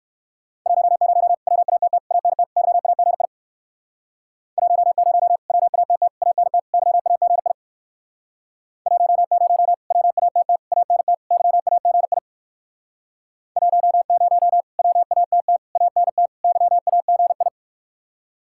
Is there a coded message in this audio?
Yes: Morse code from 0.7 to 17.5 s, heard as a keyed tone, about 65 wpm, 700 Hz. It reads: 10 WATT ANT YAGI 10 WATT ANT YAGI 10 WATT ANT YAGI 10 WATT ANT YAGI